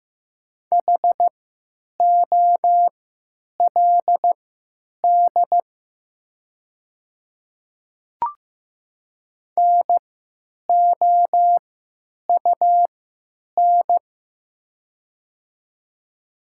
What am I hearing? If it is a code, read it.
Morse code, heard as a keyed tone, about 15 wpm, spaced as for 9 wpm, 700 Hz. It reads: HOLD NOUN
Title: 3x Character Spacing - Top 500 Words - 15wpm